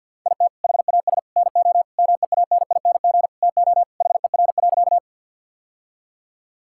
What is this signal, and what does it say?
Morse code, heard as a keyed tone, about 50 wpm, 700 Hz. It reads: IT HAS NO MEANING TO HER.
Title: Sentences from Top 1000 Words - Review Part 6 - 50wpm